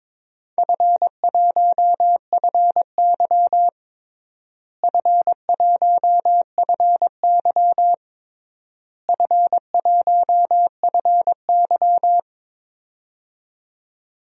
Morse code, heard as a keyed tone, about 22 wpm, 700 Hz. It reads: F1FY F1FY F1FY
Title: DXCC-100 Call Signs - 4 Characters - Repeated 3x - 22wpm